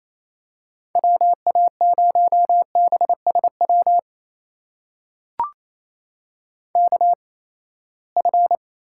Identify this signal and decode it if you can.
Morse code, heard as a keyed tone, about 28 wpm, spaced as for 19 wpm, 700 Hz. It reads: WA0BSW KF
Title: USA Call Signs - 6 Characters - 8x Character Spacing - 28wpm